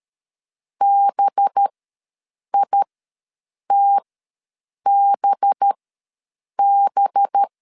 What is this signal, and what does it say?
Morse code, heard as a keyed tone, about 13 wpm, spaced as for 7 wpm, 800 Hz. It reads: BITBB
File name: 5 LB ws 12wpm